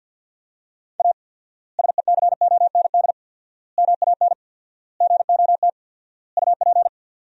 Morse code, heard as a keyed tone, about 50 wpm, 700 Hz. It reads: A SECOND MAN GOT UP
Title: 4x Word-Spacing - Sentences from Top 300 Words - 50wpm